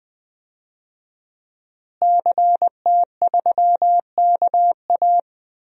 Morse code, heard as a keyed tone, about 20 wpm, 700 Hz. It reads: CT3KA
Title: DXCC-100 Call Signs - 5 Characters - Repeated 3x - 20wpm